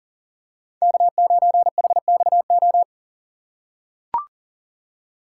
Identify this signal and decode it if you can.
Morse code, heard as a keyed tone, about 40 wpm, 700 Hz. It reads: K9HXO
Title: USA Call Signs - 5 Characters - 4x Character Spacing - 40wpm